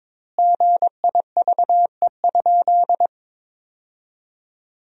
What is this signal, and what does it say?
Morse code, heard as a keyed tone, about 22 wpm, 700 Hz. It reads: GIVE?
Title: 4x Word-Spacing - Sentences from Top 200 Words - 22wpm